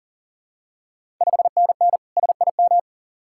Morse code, heard as a keyed tone, about 40 wpm, 700 Hz. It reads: 5NN SIM
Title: Ontario QSO Party Exchange 40wpm